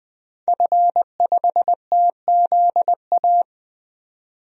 Morse code, heard as a keyed tone, about 20 wpm, 700 Hz. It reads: F5TZA
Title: DXCC-100 Call Signs - 5 Characters - 20wpm